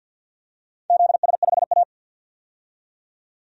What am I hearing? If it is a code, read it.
Morse code, heard as a keyed tone, about 50 wpm, 700 Hz. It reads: ZS5A
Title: DXCC-100 Call Signs - 4 Characters - 50wpm